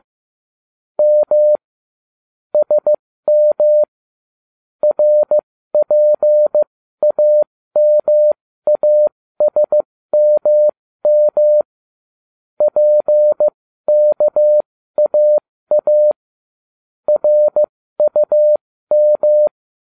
Morse code, heard as a keyed tone, about 15 wpm, spaced as for 13 wpm, 600 Hz. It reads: M SM RPAMASMM PKAA RUM